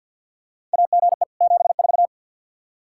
Morse code, heard as a keyed tone, about 53 wpm, 700 Hz. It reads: AGE 74
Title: K1USN SST QSO Practice 50wpm